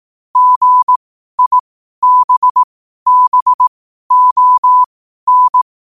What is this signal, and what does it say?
Morse code, heard as a keyed tone, about 18 wpm, spaced as for 12 wpm, 1000 Hz. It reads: GIBBON
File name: qotdfiles / 12wpm